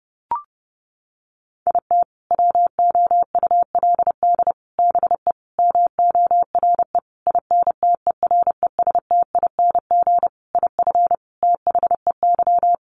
Morse code, heard as a keyed tone, about 30 wpm, 700 Hz. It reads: IT WOULD BE MORE INTERESTING IF THEY
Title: Sentences from Top 1000 Words 30wpm